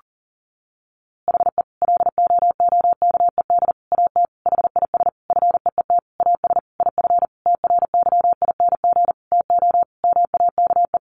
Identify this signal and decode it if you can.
Morse code, heard as a keyed tone, about 37 wpm, 700 Hz. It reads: HE LOOKED AT HIS FEET AS IF TRYING TO MAKE
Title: A Sound Of Thunder 40wpm